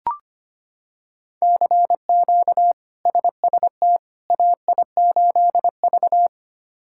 Morse code, heard as a keyed tone, about 25 wpm, 700 Hz. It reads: CQ SST AI8V
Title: K1USN SST QSO Practice - 25wpm